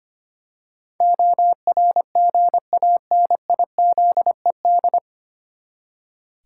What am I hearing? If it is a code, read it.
Morse code, heard as a keyed tone, about 25 wpm, 700 Hz. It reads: ORGANIZED
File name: Nine Letter Words - 25wpm